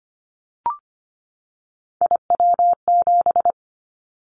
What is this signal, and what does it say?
Morse code, heard as a keyed tone, about 25 wpm, 700 Hz. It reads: IW7
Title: Callsign Prefixes for Top 25 DXCC - 25wpm